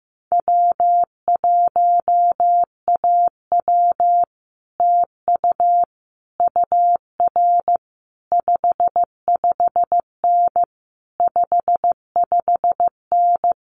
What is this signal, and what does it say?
Morse code, heard as a keyed tone, about 15 wpm, 700 Hz. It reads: W1AW TU UR 55N 55N
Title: Common QSO Phrases Encore 15wpm